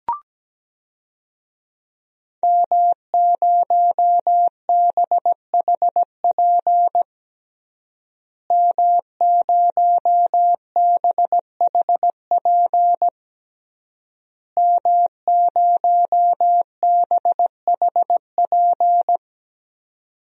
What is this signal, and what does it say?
Morse code, heard as a keyed tone, about 17 wpm, 700 Hz. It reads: M0BHP M0BHP M0BHP
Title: DXCC-100 Call Signs - 5 Characters - Repeated 3x - 17wpm